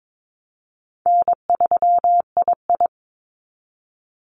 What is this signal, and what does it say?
Morse code, heard as a keyed tone, about 22 wpm, 700 Hz. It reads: N3II
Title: NA SOTA Activator Callsigns 22wpm